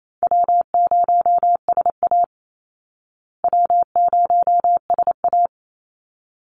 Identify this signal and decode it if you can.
Morse code, heard as a keyed tone, about 28 wpm, 700 Hz. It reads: W0SA W0SA
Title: Easy Call Signs Repeated 2x - 28wpm